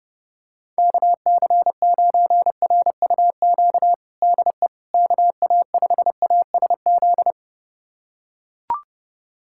Morse code, heard as a keyed tone, about 30 wpm, 700 Hz. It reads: KC9RUQ DE KA5ASZ
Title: USA Callsign DE Callsign - 6 Characters - 30wpm